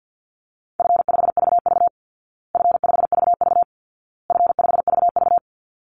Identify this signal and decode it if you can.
Morse code, heard as a keyed tone, about 50 wpm, 700 Hz. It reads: F5VV F5VV F5VV
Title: Easy Call Signs Repeated 3x - 50wpm